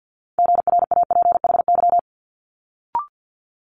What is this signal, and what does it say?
Morse code, heard as a keyed tone, about 50 wpm, 700 Hz. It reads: GRAPHY
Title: Multi-letter Phonemes Quick Fire 50wpm